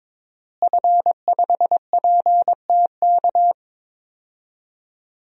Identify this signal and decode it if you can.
Morse code, heard as a keyed tone, about 22 wpm, 700 Hz. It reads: F5PTK
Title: DXCC-100 Call Signs - 5 Characters - Repeated 2x - 22wpm